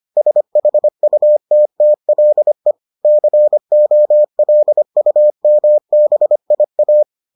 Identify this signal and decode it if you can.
Morse code, heard as a keyed tone, about 25 wpm, 600 Hz. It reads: SHUTTLE COLUMBIA